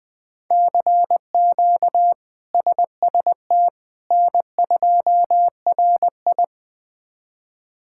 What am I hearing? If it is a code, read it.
Morse code, heard as a keyed tone, about 20 wpm, 700 Hz. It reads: CQ SST N2RI
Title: K1USN SST QSO Practice - 20wpm